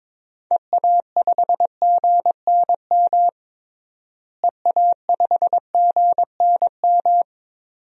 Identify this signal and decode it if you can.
Morse code, heard as a keyed tone, about 21 wpm, 700 Hz. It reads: EA5GNM EA5GNM
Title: DXCC-100 Call Signs - 6 Characters - Repeated 2x - 22wpm